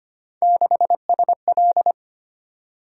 Morse code, heard as a keyed tone, about 25 wpm, 700 Hz. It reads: 6SL
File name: Three Letter-Number Combinations - 25wpm